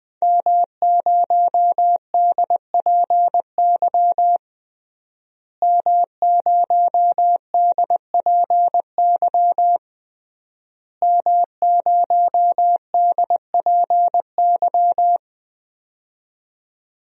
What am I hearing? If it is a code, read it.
Morse code, heard as a keyed tone, about 20 wpm, 700 Hz. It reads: M0DPY M0DPY M0DPY
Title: DXCC-100 Call Signs - 5 Characters - Repeated 3x - 20wpm